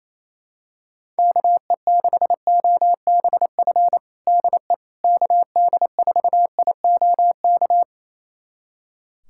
Morse code, heard as a keyed tone, about 28 wpm, 700 Hz. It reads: KE6OBF DE KD4IOK
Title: USA Callsign DE Callsign 28wpm